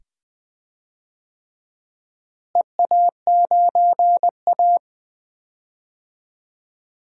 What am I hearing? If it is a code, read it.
Morse code, heard as a keyed tone, about 20 wpm, 700 Hz. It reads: EA9A
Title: DXCC-100 Call Signs - 4 Characters - Repeated 2x - 20wpm